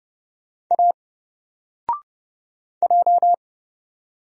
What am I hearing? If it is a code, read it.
Morse code, heard as a keyed tone, about 30 wpm, 700 Hz. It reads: A J
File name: Single-Letter-Number-30wpm